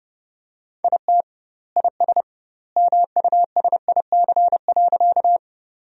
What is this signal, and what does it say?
Morse code, heard as a keyed tone, about 30 wpm, 700 Hz. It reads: IT IS MUSIC.
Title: 2x Word-Spacing - Sentences from Top 300 Words - 30wpm